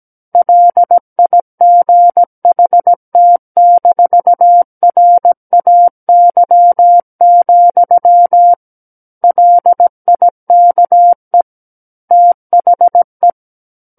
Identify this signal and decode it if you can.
Morse code, heard as a keyed tone, about 17 wpm, 700 Hz. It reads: LIGHT-RAY, LIKE THE
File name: Book One / 08-Chapter 8 - 17 WPM